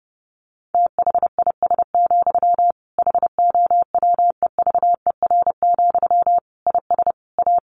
Morse code, heard as a keyed tone, about 30 wpm, 700 Hz. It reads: THIS, HOWEVER, IS A
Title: Sentences from Top 1500 Words 30wpm